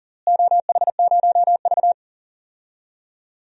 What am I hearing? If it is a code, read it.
Morse code, heard as a keyed tone, about 40 wpm, 700 Hz. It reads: OH0V
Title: DXCC-100 Call Signs - 4 Characters - Repeated 2x - 40wpm